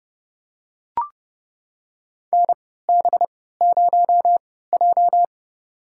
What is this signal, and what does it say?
Morse code, heard as a keyed tone, about 30 wpm, spaced as for 17 wpm, 700 Hz. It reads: NB0J
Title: USA Call Signs - 3x Character Spacing - 30wpm